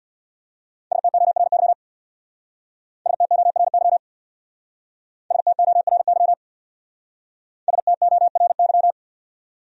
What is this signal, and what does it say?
Morse code, heard as a keyed tone, about 65 wpm, 690 Hz. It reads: STORY STORY STORY STORY